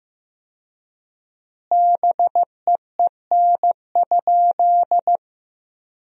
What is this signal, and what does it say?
Morse code, heard as a keyed tone, about 15 wpm, 700 Hz. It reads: BEEN?
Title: 4x Word-Spacing - Sentences from Top 400 Words - 15wpm